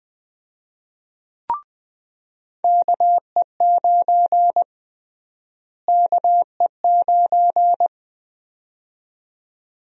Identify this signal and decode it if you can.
Morse code, heard as a keyed tone, about 20 wpm, 700 Hz. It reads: KE9 KE9
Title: USA Call Sign Prefixes - Repeated 2x - 20wpm